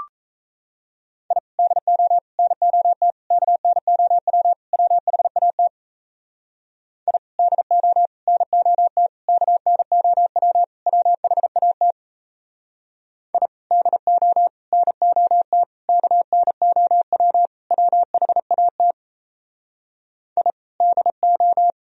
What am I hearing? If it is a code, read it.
Morse code, heard as a keyed tone, about 34 wpm, 700 Hz. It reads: I DO NOT KNOW WHAT I DO NOT KNOW WHAT I DO NOT KNOW WHAT I DO
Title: Speed-Racing - Sets of 5 Words 28wpm